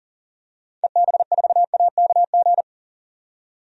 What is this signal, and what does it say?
Morse code, heard as a keyed tone, about 40 wpm, 700 Hz. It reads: EB4AKG
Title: DXCC-100 Call Signs - 6 Characters - 40wpm